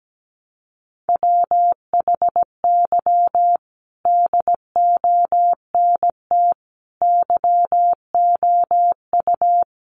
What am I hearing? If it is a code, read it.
Morse code, heard as a keyed tone, about 17 wpm, 700 Hz. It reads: WHY DONT YOU
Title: Sentences from Top 300 Words 17wpm